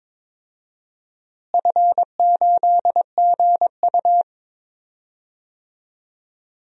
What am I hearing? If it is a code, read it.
Morse code, heard as a keyed tone, about 22 wpm, 700 Hz. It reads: F8GU
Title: DXCC-100 Call Signs - 4 Characters - Repeated 3x - 22wpm